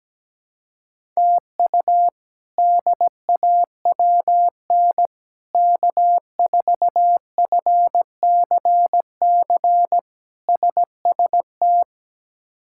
Morse code, heard as a keyed tone, about 17 wpm, 700 Hz. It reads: TU DAWN K4FCC SST
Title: K1USN SST QSO Practice - 17wpm